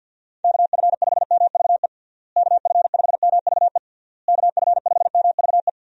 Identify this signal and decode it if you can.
Morse code, heard as a keyed tone, about 51 wpm, 700 Hz. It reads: KF5MVE KF5MVE KF5MVE
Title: USA Call Signs - 6 Characters - Repeated 3x - 50wpm